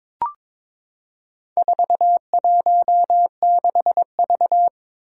Morse code, heard as a keyed tone, about 22 wpm, 700 Hz. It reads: 416V
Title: Four Challenging Letter-Number Combinations 22wpm